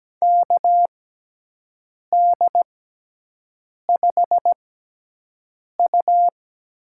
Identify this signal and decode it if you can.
Morse code, heard as a keyed tone, about 17 wpm, spaced as for 6 wpm, 700 Hz. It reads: KD5U
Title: USA Call Signs - 6 Characters - 6x Character Spacing - 17wpm